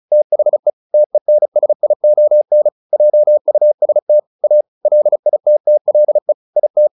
Morse code, heard as a keyed tone, about 34 wpm, 600 Hz. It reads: THE TENSION JUST A LITTLE IT
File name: the old man and the sea / the old man and the sea 1 35wpm